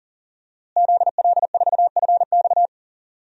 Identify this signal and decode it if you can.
Morse code, heard as a keyed tone, about 40 wpm, 700 Hz. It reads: ZL4FX
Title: DXCC-100 Call Signs - 5 Characters - Repeated 3x - 40wpm